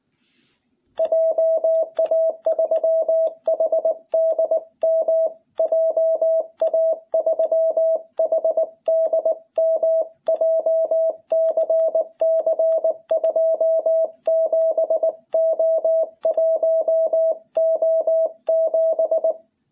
Morse code, heard as a keyed tone, about 19 wpm, 660 Hz. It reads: JA3HDM JA3HDM JCC 27O1O7
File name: jh8cbh / 2021cw